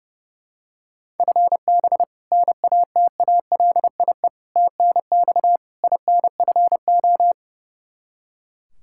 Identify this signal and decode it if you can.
Morse code, heard as a keyed tone, about 32 wpm, 700 Hz. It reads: FB NATALIE TNX INFO
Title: Practice QSO - Exchange 3 - More Info - 30wpm